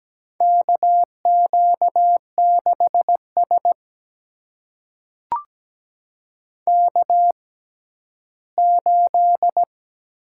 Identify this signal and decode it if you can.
Morse code, heard as a keyed tone, about 17 wpm, spaced as for 11 wpm, 700 Hz. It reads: KQ6S K8
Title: USA Call Signs - 4 Characters - 6x Character Spacing - 17wpm